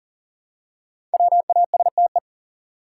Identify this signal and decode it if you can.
Morse code, heard as a keyed tone, about 40 wpm, 700 Hz. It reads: WASTE
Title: Five Letter Words - 40wpm